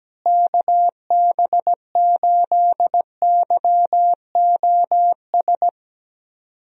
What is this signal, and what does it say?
Morse code, heard as a keyed tone, about 17 wpm, 700 Hz. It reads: KB8YOS
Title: USA Call Signs - 6 Characters - Repeated 2x - 17wpm